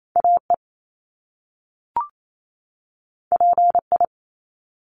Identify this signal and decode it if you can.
Morse code, heard as a keyed tone, about 28 wpm, 700 Hz. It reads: AE PI
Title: Top Two Letter Combinations in Words 28wpm